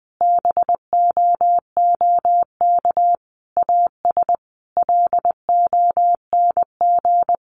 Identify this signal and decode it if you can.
Morse code, heard as a keyed tone, about 20 wpm, 700 Hz. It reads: BOOK AS LONG
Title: Sentences from Top 1200 Words 20wpm